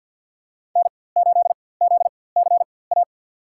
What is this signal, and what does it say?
Morse code, heard as a keyed tone, about 53 wpm, spaced as for 24 wpm, 700 Hz. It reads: N8ZCA
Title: USA Call Signs - 5 Characters - 4x Character Spacing - 50wpm